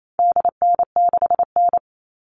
Morse code, heard as a keyed tone, about 28 wpm, 700 Hz. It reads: DN6N
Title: All Cut Numbers - 4-Digit 28wpm